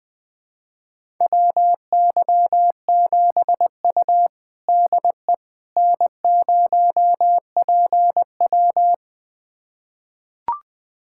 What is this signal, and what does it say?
Morse code, heard as a keyed tone, about 21 wpm, 700 Hz. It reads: WY7U DE N0PW
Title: USA Callsign DE Callsign - 4 Characters - 20wpm